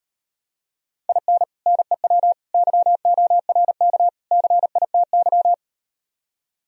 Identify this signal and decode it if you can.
Morse code, heard as a keyed tone, about 38 wpm, 700 Hz. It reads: IN NEW YORK CITY